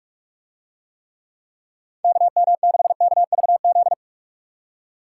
Morse code, heard as a keyed tone, about 44 wpm, 700 Hz. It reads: KM6KVZ